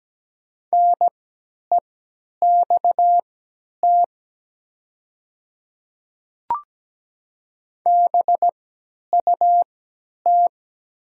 Morse code, heard as a keyed tone, about 17 wpm, spaced as for 10 wpm, 690 Hz. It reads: NEXT BUT
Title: 3x Character Spacing - Top 300 Words - 17wpm